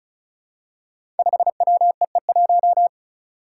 Morse code, heard as a keyed tone, about 36 wpm, 700 Hz. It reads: 5WEE1
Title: Five Challenging Letter-Number Combinations 35wpm